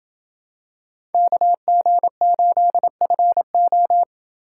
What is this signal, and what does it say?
Morse code, heard as a keyed tone, about 27 wpm, 700 Hz. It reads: KG8FO